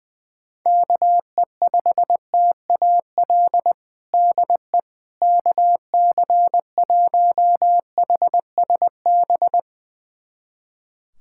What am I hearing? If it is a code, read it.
Morse code, heard as a keyed tone, about 21 wpm, 700 Hz. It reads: KE5TAL DE KC1HSB
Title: USA Callsign DE Callsign - 6 Characters - 20wpm